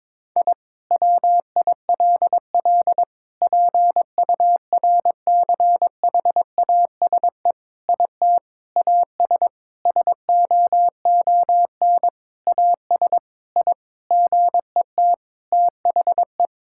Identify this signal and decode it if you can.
Morse code, heard as a keyed tone, about 21 wpm, 700 Hz. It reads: I WILL PURCHASE IT AS SOON AS I GET THE
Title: Sentences from Top 1000 Words - Review Part 5 - 22wpm